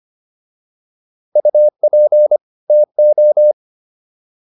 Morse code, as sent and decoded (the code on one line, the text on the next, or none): ..- .--. / - ---
UP TO